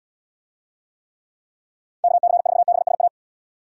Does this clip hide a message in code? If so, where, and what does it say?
Yes: Morse code from 2.0 to 3.1 s, heard as a keyed tone, about 70 wpm, 700 Hz. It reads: KC4BSA